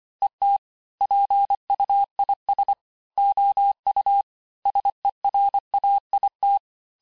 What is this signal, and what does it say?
Morse code, heard as a keyed tone, about 25 wpm, 800 Hz. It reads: ET PUIS OU SERAIT
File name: cours-868-teste-clair-vitesse-23mpm